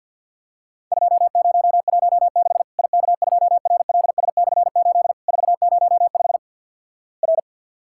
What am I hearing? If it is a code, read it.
Morse code, heard as a keyed tone, about 50 wpm, 700 Hz. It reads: 2016 IK2RLS/8 405 R
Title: Worked All Europe DX Contest - QTC Exchange - 50wpm